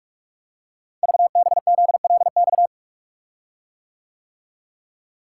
Morse code, heard as a keyed tone, about 44 wpm, 700 Hz. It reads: VBZLX
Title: Code Groups - 5 Letters - 45wpm